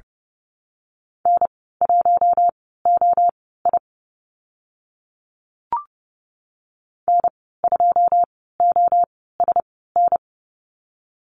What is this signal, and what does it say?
Morse code, heard as a keyed tone, about 30 wpm, spaced as for 17 wpm, 700 Hz. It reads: N1OI N2OSN
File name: Lesson 15g - Number 5 - Call Signs 30wpm